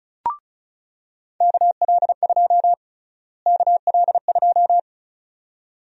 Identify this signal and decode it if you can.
Morse code, heard as a keyed tone, about 36 wpm, 700 Hz. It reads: KL2 KL2
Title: USA Call Sign Prefixes - Repeated 2x - 35wpm